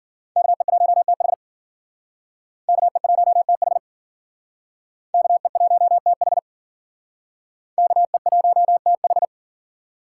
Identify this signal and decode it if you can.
Morse code, heard as a keyed tone, about 52 wpm, 700 Hz. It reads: KE1TH KE1TH KE1TH KE1TH